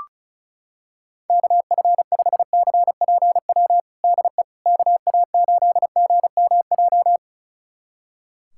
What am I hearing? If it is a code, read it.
Morse code, heard as a keyed tone, about 34 wpm, 700 Hz. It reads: KF5CPW DE KA8GMJ
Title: USA Callsign DE Callsign - 6 Characters - 35wpm